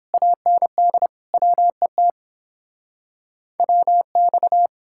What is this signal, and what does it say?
Morse code, heard as a keyed tone, about 29 wpm, 700 Hz. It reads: AND WET WX